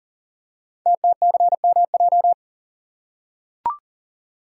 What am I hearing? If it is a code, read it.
Morse code, heard as a keyed tone, about 39 wpm, 690 Hz. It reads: TTCMJ
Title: Code Groups - 5 Letters - 40wpm